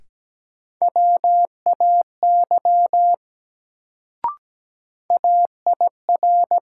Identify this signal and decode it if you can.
Morse code, heard as a keyed tone, about 17 wpm, 700 Hz. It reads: WAY AIR
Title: Sets of 2 Words - New Word Drill - 17wpm